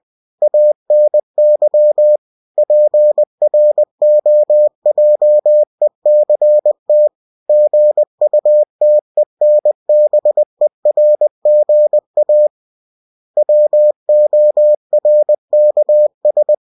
Morse code, heard as a keyed tone, about 20 wpm, 600 Hz. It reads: ANY PROJECT GUTENBERGA WORKS
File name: cw-books / winnie-the-pooh / winnie-20wpm-page-0116